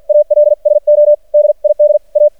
Morse code, heard as a keyed tone, about 50 wpm, spaced as for 42 wpm, 600 Hz. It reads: ALSO DID A
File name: archived / CRSnet